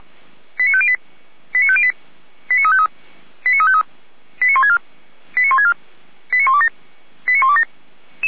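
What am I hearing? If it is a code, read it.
Five-tone selective call (ZVEI), heard as quick runs of five tones, tones of about 80 ms. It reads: 89508 89508 89353 89353 89164 89164 89127 89127
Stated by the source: ZVEI five-tone selective call 89126 received by SDR.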